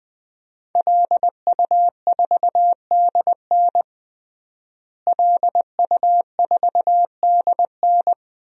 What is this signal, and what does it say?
Morse code, heard as a keyed tone, about 21 wpm, 700 Hz. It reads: LU4DN LU4DN
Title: DXCC-100 Call Signs - 5 Characters - Repeated 3x - 20wpm